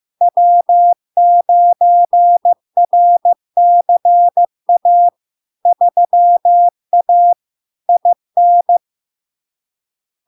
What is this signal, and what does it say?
Morse code, heard as a keyed tone, about 15 wpm, 700 Hz. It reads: W9RCA 3A IN
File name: LARK-FD-2022-Log-in-CW-15wpm-700-Hz